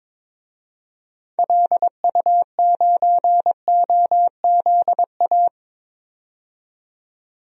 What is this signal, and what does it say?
Morse code, heard as a keyed tone, about 22 wpm, 700 Hz. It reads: LU9OZA
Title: DXCC-100 Call Signs - 6 Characters - 22wpm